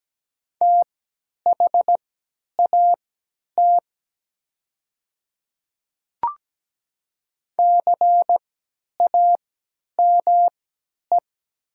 Morse code, heard as a keyed tone, about 17 wpm, spaced as for 10 wpm, 700 Hz. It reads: THAT CAME
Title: 3x Character Spacing - Top 200 Words - 17wpm